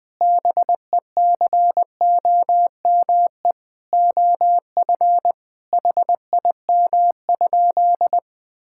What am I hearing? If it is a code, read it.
Morse code, heard as a keyed tone, about 20 wpm, 700 Hz. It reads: BECOME OF HIM?
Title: Sentences from Top 1000 Words - Review Part 5 - 20wpm